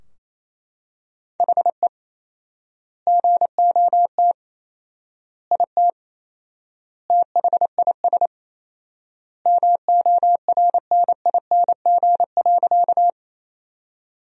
Morse code, heard as a keyed tone, about 28 wpm, 700 Hz. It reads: HE GOT IT THIS MORNING.
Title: 4x Word-Spacing - Sentences from Top 400 Words - 28wpm